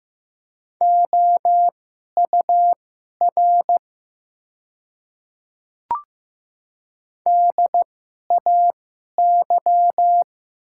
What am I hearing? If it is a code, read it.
Morse code, heard as a keyed tone, about 15 wpm, spaced as for 11 wpm, 700 Hz. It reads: OUR DAY
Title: 2x Character Spacing - Top 300 Words - 15wpm